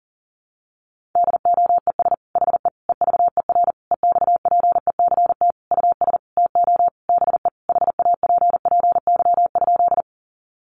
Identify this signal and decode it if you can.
Morse code, heard as a keyed tone, about 40 wpm, 700 Hz. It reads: DOES HE EVER EXPECT US TO BE HAPPY?
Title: Sentences from Top 1100 Words 40wpm